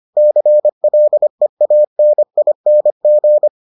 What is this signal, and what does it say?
Morse code, heard as a keyed tone, about 25 wpm, 600 Hz. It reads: CLEANING